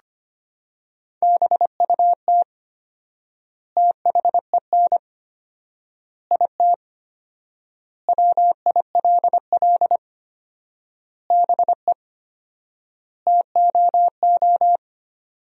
Morse code, heard as a keyed tone, about 25 wpm, 700 Hz. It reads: BUT THEN IT WILL BE TOO
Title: 4x Word-Spacing - Sentences from Top 300 Words - 25wpm